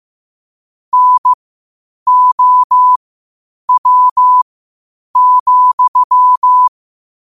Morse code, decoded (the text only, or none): NOW,